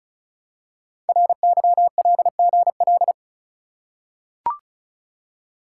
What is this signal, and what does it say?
Morse code, heard as a keyed tone, about 36 wpm, 700 Hz. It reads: RYLGL
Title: Code Groups - 5 Letters - 35wpm